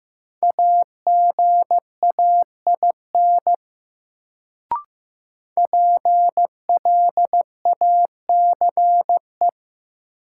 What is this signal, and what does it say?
Morse code, heard as a keyed tone, about 15 wpm, 700 Hz. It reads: AGAIN PLACE
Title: Sets of 4 Words - New Word Drill - 15wpm